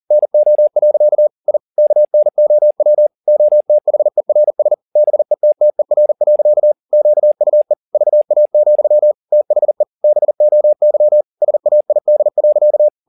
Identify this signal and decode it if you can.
Morse code, heard as a keyed tone, about 40 wpm, 600 Hz. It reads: NO. I KNOW OTHERS BETTER. QUE VA, THE BOY SAID.